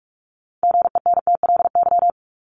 Morse code, heard as a keyed tone, about 44 wpm, 700 Hz. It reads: GENTLY
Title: Top 1251-1500 Words 45wpm